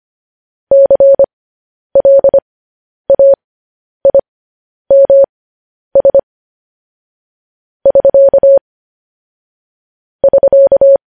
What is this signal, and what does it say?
Morse code, heard as a keyed tone, about 25 wpm, spaced as for 10 wpm, 550 Hz. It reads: CLAIMS <SK> <SK>